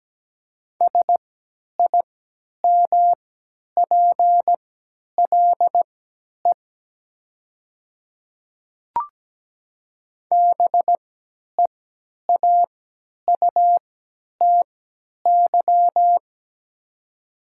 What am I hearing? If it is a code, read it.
Morse code, heard as a keyed tone, about 17 wpm, spaced as for 10 wpm, 690 Hz. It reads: SIMPLE BEAUTY
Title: 3x Character Spacing - Top 500 Words - 17wpm